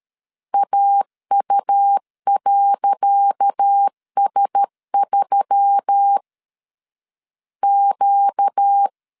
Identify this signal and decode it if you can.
Morse code, heard as a keyed tone, about 13 wpm, 800 Hz. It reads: AU.S3 Q
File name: LEZIONI / 10 SIGNS / 10C DOT 12wpm 2451